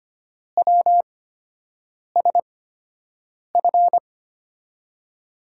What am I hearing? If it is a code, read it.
Morse code, heard as a keyed tone, about 25 wpm, spaced as for 7 wpm, 700 Hz. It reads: WSF
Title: USA Call Signs - 8x Character Spacing - 25wpm